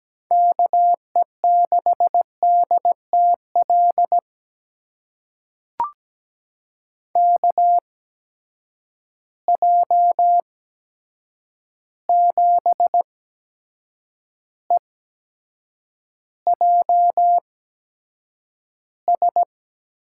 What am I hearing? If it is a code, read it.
Morse code, heard as a keyed tone, about 17 wpm, spaced as for 7 wpm, 700 Hz. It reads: KE6DTL KJ7EJS
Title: USA Call Signs - 8x Character Spacing - 17wpm